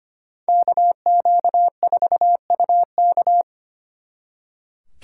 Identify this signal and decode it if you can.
Morse code, heard as a keyed tone, about 30 wpm, 700 Hz. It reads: KQ4UK
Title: Speed Racing - USA 2x2 Call Signs 25wpm